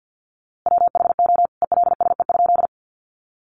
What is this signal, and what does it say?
Morse code, heard as a keyed tone, about 48 wpm, 700 Hz. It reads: WHO ELSE?
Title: A Sound Of Thunder 50wpm